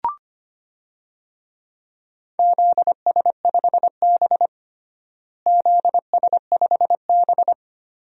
Morse code, heard as a keyed tone, about 25 wpm, 700 Hz. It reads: ZS5B ZS5B
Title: DXCC-100 Call Signs - 4 Characters - Repeated 2x - 25wpm